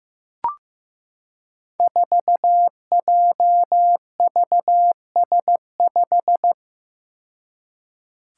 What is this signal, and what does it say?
Morse code, heard as a keyed tone, about 15 wpm, 700 Hz. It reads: 4JVS5
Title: Five Challenging Letter-Number Combinations 15wpm